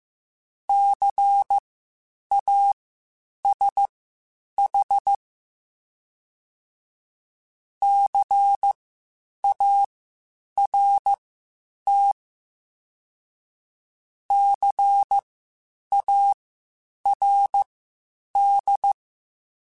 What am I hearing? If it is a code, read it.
Morse code, heard as a keyed tone, about 15 wpm, spaced as for 8 wpm, 800 Hz. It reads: CASH CART CARD
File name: Spaced Alphas / Spaced Alphas mp3s / Cs sp @13